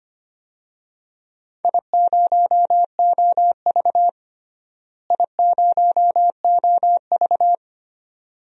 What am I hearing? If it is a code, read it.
Morse code, heard as a keyed tone, about 25 wpm, 700 Hz. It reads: I0OV I0OV
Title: DXCC-100 Call Signs - 4 Characters - Repeated 2x - 25wpm